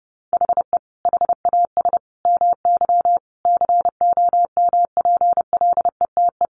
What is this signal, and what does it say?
Morse code, heard as a keyed tone, about 30 wpm, 700 Hz. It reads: HE HAS MY COMPLETE
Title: Sentences from Top 1500 Words 30wpm